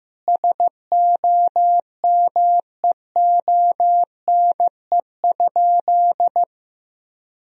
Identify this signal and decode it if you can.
Morse code, heard as a keyed tone, about 15 wpm, 700 Hz. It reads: SOMEONE?
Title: Sentences from Top 1000 Words - Review Part 4 - 15wpm